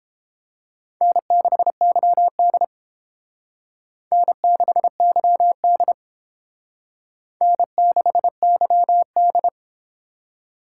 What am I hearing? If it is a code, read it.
Morse code, heard as a keyed tone, about 28 wpm, 700 Hz. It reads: N6YD N6YD N6YD